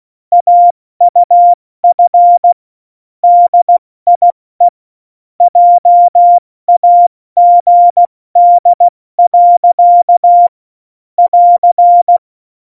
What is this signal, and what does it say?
Morse code, heard as a keyed tone, about 16 wpm, spaced as for 14 wpm, 700 Hz. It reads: AUF DIE JAGD. <AR>
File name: Morsekurs / Witze / Witz 27 Juni 17, Tempo 70 F80